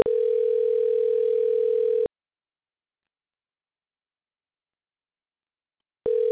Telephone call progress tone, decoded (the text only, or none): ringback tone